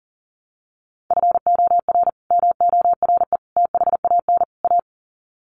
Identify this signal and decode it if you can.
Morse code, heard as a keyed tone, about 40 wpm, 700 Hz. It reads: FOR MORE THAN A
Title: Sets of 4 Words 40wpm